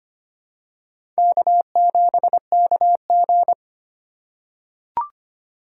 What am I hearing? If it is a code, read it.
Morse code, heard as a keyed tone, about 25 wpm, 700 Hz. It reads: K7KG
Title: USA Call Signs - 4 Characters - 4x Character Spacing - 25wpm